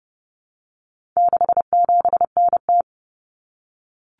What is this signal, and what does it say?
Morse code, heard as a keyed tone, about 30 wpm, 700 Hz. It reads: 67NT
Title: Common Cut Numbers - 4-Digit 30wpm